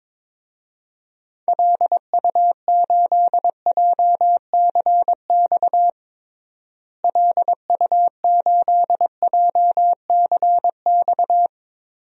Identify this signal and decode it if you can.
Morse code, heard as a keyed tone, about 22 wpm, 700 Hz. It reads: LU8JCX LU8JCX
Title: DXCC-100 Call Signs - 6 Characters - Repeated 3x - 22wpm